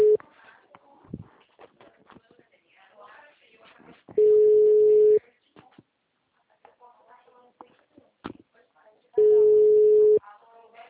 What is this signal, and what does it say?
Ringback tone (Europe, 425 Hz), with speech.